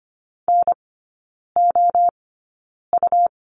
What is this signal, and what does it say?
Morse code, heard as a keyed tone, about 25 wpm, spaced as for 9 wpm, 700 Hz. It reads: NOU
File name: Lesson 17e - Letter u - Three Characters 25wpm